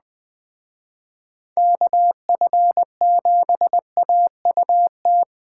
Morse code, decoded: KF7AUT

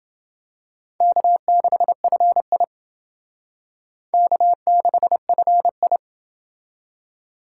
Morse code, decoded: K6FI K6FI

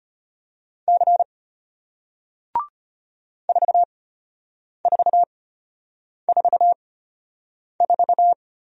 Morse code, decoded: C 4444

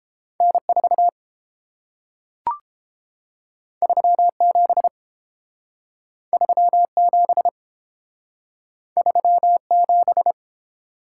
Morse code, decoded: N 4 3 7 3 7 3 7

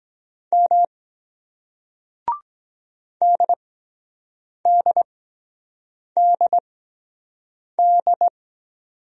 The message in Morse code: M DDDD

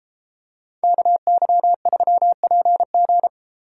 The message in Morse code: KY3PG

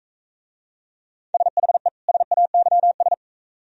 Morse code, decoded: SHE SAYS

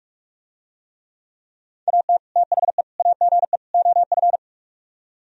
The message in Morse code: AT THE AGE OF

